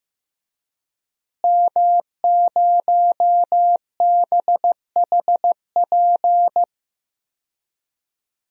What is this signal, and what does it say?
Morse code, heard as a keyed tone, about 16 wpm, 700 Hz. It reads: M0BHP
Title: DXCC-100 Call Signs - 5 Characters - Repeated 3x - 15wpm